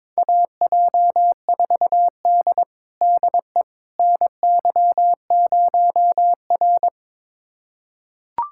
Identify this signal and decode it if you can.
Morse code, heard as a keyed tone, about 22 wpm, 700 Hz. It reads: AJ4D DE NY0R
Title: USA Callsign DE Callsign - 4 Characters - 22wpm